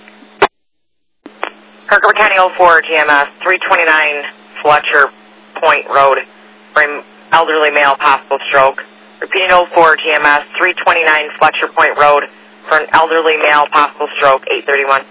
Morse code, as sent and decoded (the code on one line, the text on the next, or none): none